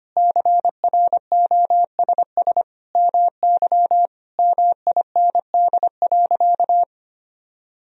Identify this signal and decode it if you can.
Morse code, heard as a keyed tone, about 25 wpm, 700 Hz. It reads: CROSS MY MIND.
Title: Sentences from Top 1000 Words - Review Part 1 - 25wpm